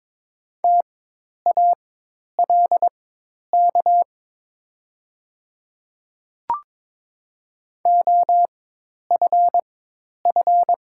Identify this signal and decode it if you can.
Morse code, heard as a keyed tone, about 23 wpm, spaced as for 10 wpm, 700 Hz. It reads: TALK OFF
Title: 4x Character Spacing - Top 500 Words - 22wpm